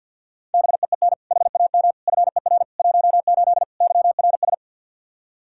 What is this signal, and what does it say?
Morse code, heard as a keyed tone, about 51 wpm, 700 Hz. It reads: BEEN HAM FER 18 YRS
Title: Practice QSO - Exchange 3 - More Info - 50wpm